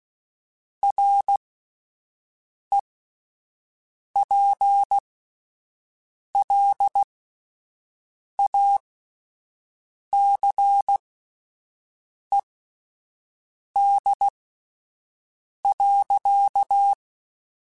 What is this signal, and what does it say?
Morse code, heard as a keyed tone, about 16 wpm, spaced as for 5 wpm, 800 Hz. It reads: REPLACED.
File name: Spaced / Chuckles Spaced-mp3s / Squeaky sp @10